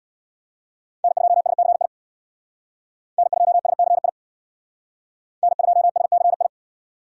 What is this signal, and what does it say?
Morse code, heard as a keyed tone, about 62 wpm, 700 Hz. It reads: N2SCI N2SCI N2SCI
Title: Speed Racing - USA 1x3 Call Signs 50wpm